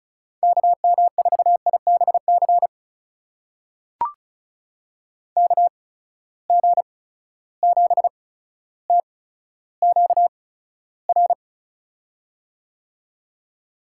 Morse code, heard as a keyed tone, about 36 wpm, spaced as for 15 wpm, 700 Hz. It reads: KM4IBC KG7TQR
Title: USA Call Signs - 6 Characters - 8x Character Spacing - 35wpm